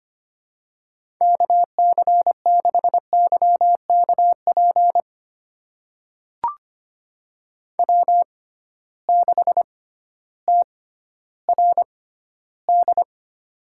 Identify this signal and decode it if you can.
Morse code, heard as a keyed tone, about 25 wpm, spaced as for 13 wpm, 700 Hz. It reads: KC6YKP W6TRD
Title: USA Call Signs - 6x Character Spacing - 25wpm